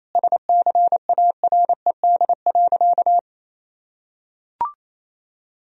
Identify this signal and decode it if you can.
Morse code, heard as a keyed tone, about 28 wpm, 700 Hz. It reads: SCARED.
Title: Sentences from Top 1000 Words - Review Part 4 - 28wpm